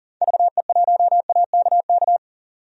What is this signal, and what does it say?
Morse code, heard as a keyed tone, about 40 wpm, 700 Hz. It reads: VE1AKK
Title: DXCC-100 Call Signs - 6 Characters - 40wpm